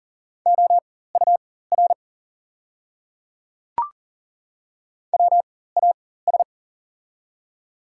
Morse code, heard as a keyed tone, about 38 wpm, spaced as for 18 wpm, 700 Hz. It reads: OUR WAS
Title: 4x Character Spacing - Top 200 Words - 40wpm